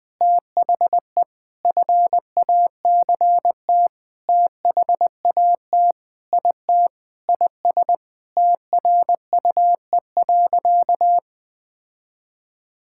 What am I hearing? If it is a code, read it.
Morse code, heard as a keyed tone, about 20 wpm, 700 Hz. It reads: THE FACT THAT IT IS TRUE.
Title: Sentences with 4-Word Phrases - Spoken Beginning - 20wpm